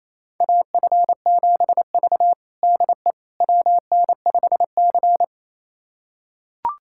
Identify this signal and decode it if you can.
Morse code, heard as a keyed tone, about 28 wpm, 700 Hz. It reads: AF7V DE WN5C
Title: USA Callsign DE Callsign - 4 Characters - 28wpm